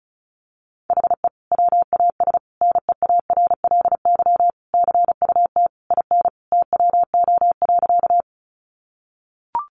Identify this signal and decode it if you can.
Morse code, heard as a keyed tone, about 34 wpm, 700 Hz. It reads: HE WAS NEARLY CUT IN TWO.
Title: Sentences from Top 1300 Words 35wpm